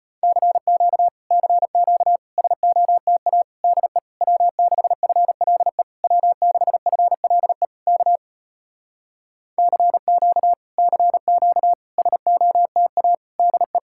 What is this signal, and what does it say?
Morse code, heard as a keyed tone, about 33 wpm, 700 Hz. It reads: CQ CQ SOTA DE W6FLE W6FLE K CQ CQ SOTA DE